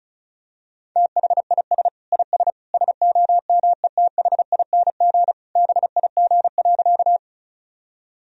Morse code, heard as a keyed tone, about 36 wpm, 700 Hz. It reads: THIS IS SOMETHING BIG.
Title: Sentences from Top 1000 Words - Review Part 1 - 35wpm